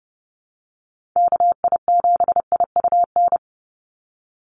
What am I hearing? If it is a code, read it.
Morse code, heard as a keyed tone, about 30 wpm, 700 Hz. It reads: KI7IUN
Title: USA Call Signs 30wpm